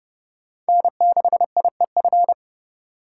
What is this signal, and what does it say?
Morse code, heard as a keyed tone, about 30 wpm, 700 Hz. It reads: N6IEF